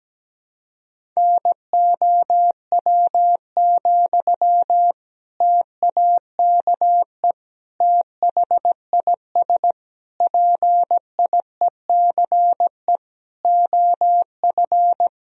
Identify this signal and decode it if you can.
Morse code, heard as a keyed tone, about 17 wpm, 700 Hz. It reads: NOW, TAKE THIS PIECE OF
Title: Sentences from Top 400 Words and Abbreviations 17wpm